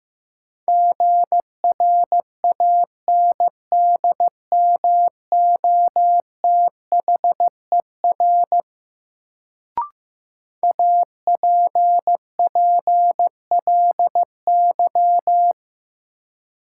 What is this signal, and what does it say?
Morse code, heard as a keyed tone, about 15 wpm, 700 Hz. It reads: GRANDMOTHER APPLY
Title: Sentences from Top 1200 Words - New Word Drill Alt - 15wpm